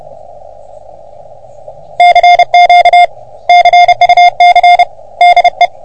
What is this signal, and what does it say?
Morse code, heard as a keyed tone, about 30 wpm, 710 Hz. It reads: CQ CUC DE
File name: OM3PA PILUP CUC 388